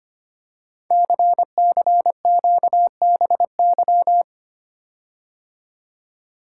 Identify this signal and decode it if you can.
Morse code, heard as a keyed tone, about 25 wpm, 700 Hz. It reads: CCQBY
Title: Code Groups - 5 Letters - 25wpm